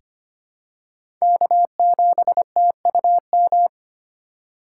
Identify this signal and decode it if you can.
Morse code, heard as a keyed tone, about 25 wpm, 700 Hz. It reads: K7TUM
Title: USA Call Signs - 5 Characters - 3x Character Spacing - 25wpm